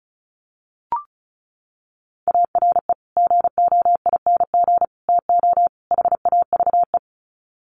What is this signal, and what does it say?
Morse code, heard as a keyed tone, about 34 wpm, 700 Hz. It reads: ARE GOING TO HAVE
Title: Sets of 4 Words 35wpm